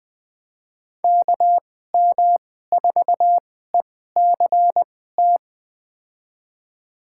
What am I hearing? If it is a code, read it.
Morse code, heard as a keyed tone, about 20 wpm, spaced as for 14 wpm, 700 Hz. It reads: KM4ECT
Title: USA Call Signs - 6 Characters - 2x Character Spacing - 20wpm